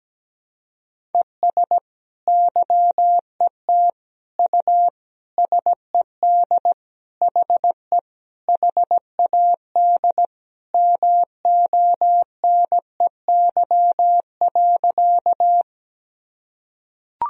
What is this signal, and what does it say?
Morse code, heard as a keyed tone, about 17 wpm, 700 Hz. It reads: ES YET U SED HE HAD MONEY.
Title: Sentences from Top 500 Words and Abbreviations 17wpm